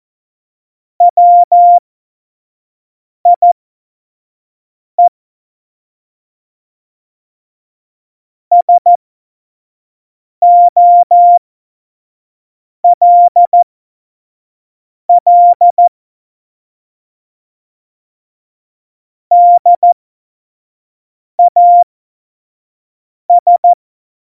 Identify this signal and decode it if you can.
Morse code, heard as a keyed tone, about 14 wpm, spaced as for 5 wpm, 700 Hz. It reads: WIE SOLL DAS
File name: Morsekurs / Witze / Witz 01. Dezember, 2020 25 F70